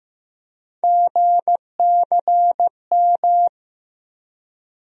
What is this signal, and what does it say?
Morse code, heard as a keyed tone, about 15 wpm, 700 Hz. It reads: GCM
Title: USA Call Sign Postfixes - 3 Letters 15wpm